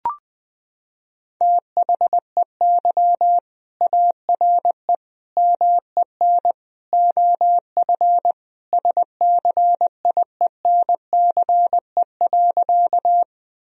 Morse code, heard as a keyed tone, about 20 wpm, 700 Hz. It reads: THEY ARE MEN OF SCIENCE.
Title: Sentences from Top 1000 Words - Review Part 2 - 20wpm